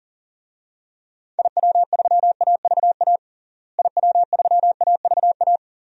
Morse code, heard as a keyed tone, about 40 wpm, 700 Hz. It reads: IW3AVA IW3AVA
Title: DXCC-100 Call Signs - 6 Characters - Repeated 2x - 40wpm